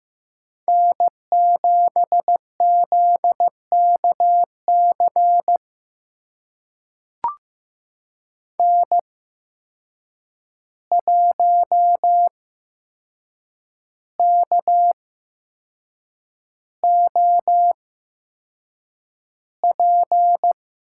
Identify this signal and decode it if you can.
Morse code, heard as a keyed tone, about 15 wpm, spaced as for 6 wpm, 700 Hz. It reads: N7ZKC N1KOP
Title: USA Call Signs - 5 Characters - 8x Character Spacing - 15wpm